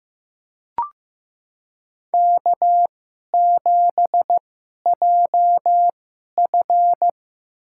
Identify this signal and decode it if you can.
Morse code, heard as a keyed tone, about 15 wpm, spaced as for 11 wpm, 700 Hz. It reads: K7JF
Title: USA Call Signs - 2x Character Spacing - 15wpm